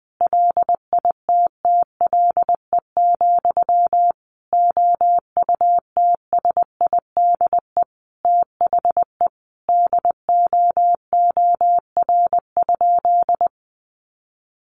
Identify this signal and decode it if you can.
Morse code, heard as a keyed tone, about 20 wpm, 700 Hz. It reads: LITTLE, OUTSIDE THE DOOR?
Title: Sentences from Top 900 Words 20wpm